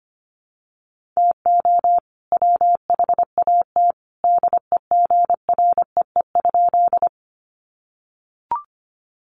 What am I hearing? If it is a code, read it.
Morse code, heard as a keyed tone, about 25 wpm, 700 Hz. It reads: TO WHAT DEGREE?
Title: Sentences from Top 1500 Words 25wpm